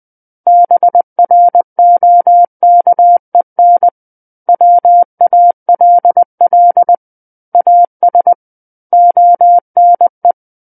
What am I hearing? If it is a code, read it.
Morse code, heard as a keyed tone, about 20 wpm, 700 Hz. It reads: BROKEN WALL AS ONE
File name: 20 WPM